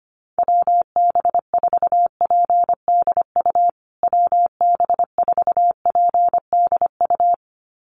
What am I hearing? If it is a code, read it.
Morse code, heard as a keyed tone, about 25 wpm, 700 Hz. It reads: WB4PDU WB4PDU
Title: USA Call Signs Repeated 2x 25wpm